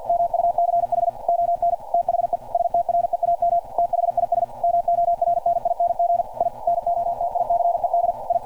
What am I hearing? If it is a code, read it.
Morse code, heard as a keyed tone, about 47 wpm, 690 Hz. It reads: MUCH ON THE STREAM THIS MORNING I NOTICED I
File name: archived / CRSnet